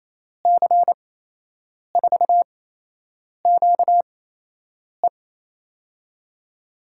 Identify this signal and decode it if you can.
Morse code, heard as a keyed tone, about 28 wpm, spaced as for 8 wpm, 700 Hz. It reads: C4QE
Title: USA Call Signs - 5 Characters - 8x Character Spacing - 28wpm